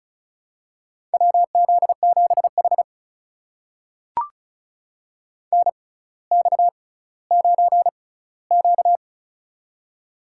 Morse code, heard as a keyed tone, about 36 wpm, spaced as for 18 wpm, 700 Hz. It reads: WZ7H NX9Q
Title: USA Call Signs - 4 Characters - 6x Character Spacing - 35wpm